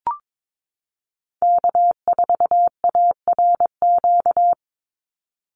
Morse code, heard as a keyed tone, about 22 wpm, 700 Hz. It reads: K4ARQ
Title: Call Signs Repeated 3x - 22wpm